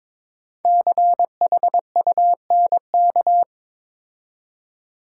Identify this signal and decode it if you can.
Morse code, heard as a keyed tone, about 23 wpm, 700 Hz. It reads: CHUNK
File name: 5 Letter Words - First 3 Letters Given - 22wpm